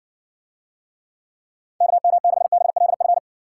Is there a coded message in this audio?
Yes: Morse code from 1.8 to 3.2 s, heard as a keyed tone, about 56 wpm, 700 Hz. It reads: KM6BLF